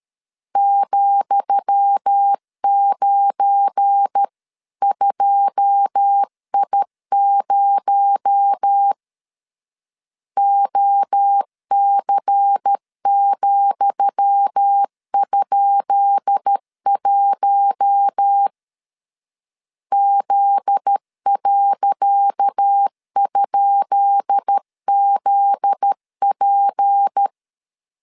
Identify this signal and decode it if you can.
Morse code, heard as a keyed tone, about 13 wpm, 800 Hz. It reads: ,92I0 OC,?1 Z.?ZP
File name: LEZIONI / 10 SIGNS / 10D COMA 12wpm 2544